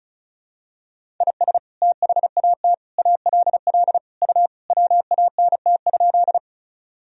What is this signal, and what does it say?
Morse code, heard as a keyed tone, about 36 wpm, 700 Hz. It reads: IS THAT ALL U WANT?
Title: Sentences from Top 200 Words and Abbreviations 35wpm